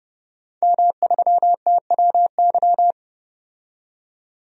Morse code, heard as a keyed tone, about 30 wpm, 700 Hz. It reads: M3TWY
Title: DXCC-100 Call Signs - 5 Characters - 30wpm